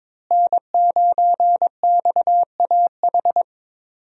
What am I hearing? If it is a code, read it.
Morse code, heard as a keyed tone, about 22 wpm, 700 Hz. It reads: N9XAH
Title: USA Call Signs - 5 Characters - Repeated 2x - 22wpm